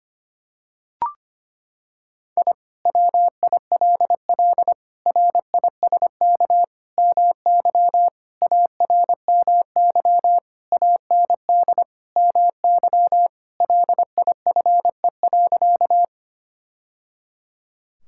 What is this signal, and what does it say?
Morse code, heard as a keyed tone, about 25 wpm, 700 Hz. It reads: I WILL RISK MY ARMY AND MY LIFE.
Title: Sentences from Top 1000 Words - Review Part 5 - 25wpm